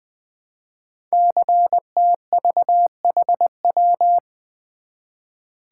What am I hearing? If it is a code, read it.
Morse code, heard as a keyed tone, about 20 wpm, 700 Hz. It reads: CTVHW
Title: Code Groups - 5 Letters - 20wpm